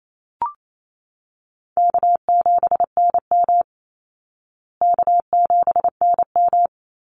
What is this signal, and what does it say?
Morse code, heard as a keyed tone, about 28 wpm, 700 Hz. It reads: K7NM K7NM
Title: Call Signs Repeated 3x - 28wpm